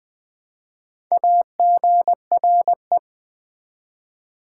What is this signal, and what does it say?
Morse code, heard as a keyed tone, about 20 wpm, 700 Hz. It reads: AGRE
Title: Build A Word - 1 Letter At A Time - 3 to 5-Letter Words - 20wpm